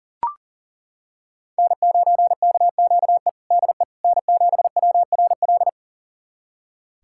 Morse code, heard as a keyed tone, about 37 wpm, 700 Hz. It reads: N9KQE DE N7WRL
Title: USA Callsign DE Callsign - 5 Characters - 40wpm